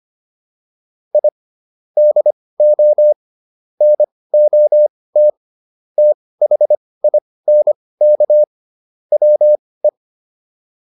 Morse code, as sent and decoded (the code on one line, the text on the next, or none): .. / -.. --- / -. --- - / - .... .. -. -.- / .-- .
I DO NOT THINK WE